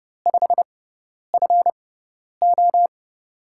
Morse code, heard as a keyed tone, about 30 wpm, spaced as for 10 wpm, 700 Hz. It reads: 5FO
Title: USA Call Signs - 6x Character Spacing - 30wpm